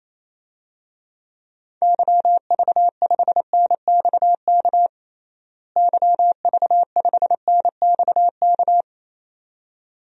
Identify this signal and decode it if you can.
Morse code, heard as a keyed tone, about 28 wpm, 700 Hz. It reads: YV5NXK YV5NXK
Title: DXCC-100 Call Signs - 6 Characters - Repeated 2x - 28wpm